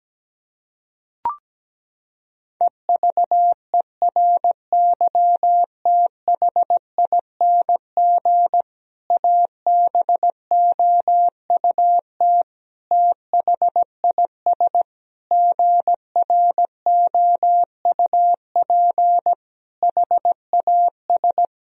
Morse code, heard as a keyed tone, about 17 wpm, 700 Hz. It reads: EVERYTHING ABOUT THIS GROUP HAS
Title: Sentences from Top 1000 Words - Review Part 1 - 17wpm